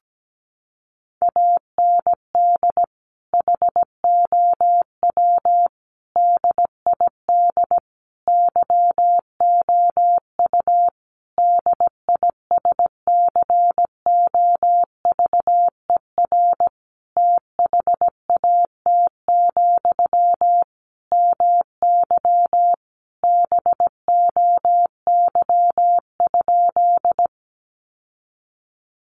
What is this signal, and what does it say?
Morse code, heard as a keyed tone, about 17 wpm, 700 Hz. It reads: AND HOW DID YOU DISCOVER THAT, MY BOY?
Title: Sentences from Top 1100 Words 17wpm